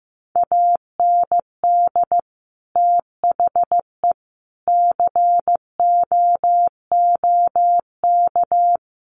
Morse code, heard as a keyed tone, about 15 wpm, 700 Hz. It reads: AND THE COOK
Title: Sentences from Top 1300 Words 15wpm